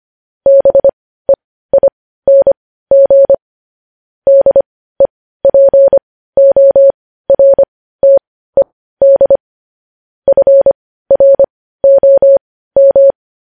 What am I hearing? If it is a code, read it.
Morse code, heard as a keyed tone, about 25 wpm, spaced as for 15 wpm, 550 Hz. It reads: BEING DEPORTED FROM
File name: podcast / News headlines-1x-25x15